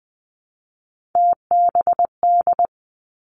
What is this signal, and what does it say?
Morse code, heard as a keyed tone, about 20 wpm, 700 Hz. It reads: TBD
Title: Three Challenging Letter-Number Combinations 20wpm